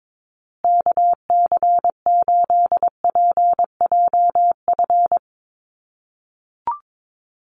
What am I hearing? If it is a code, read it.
Morse code, heard as a keyed tone, about 22 wpm, 700 Hz. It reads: KC8PJF
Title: USA Call Signs Repeated 3x 22wpm